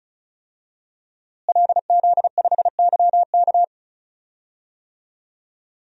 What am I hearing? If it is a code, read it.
Morse code, heard as a keyed tone, about 34 wpm, 700 Hz. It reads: LZ5YK
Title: DXCC-100 Call Signs - 5 Characters - Repeated 2x - 35wpm